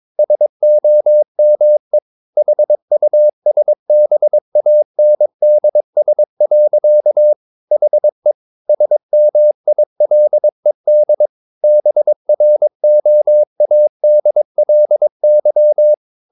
Morse code, heard as a keyed tone, about 22 wpm, 600 Hz. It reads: SOME HUSBANDS. HE SMILED BROADLY
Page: Romanen Never af Ken Follet på engelsk.